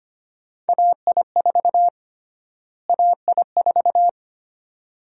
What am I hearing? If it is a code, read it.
Morse code, heard as a keyed tone, about 25 wpm, 700 Hz. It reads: AI4 AI4
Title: USA Call Sign Prefixes - Repeated 2x - 25wpm